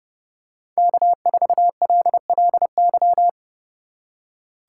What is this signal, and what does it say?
Morse code, heard as a keyed tone, about 30 wpm, 700 Hz. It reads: K4LLY
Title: USA Call Signs - 5 Characters - 2x Character Spacing - 30wpm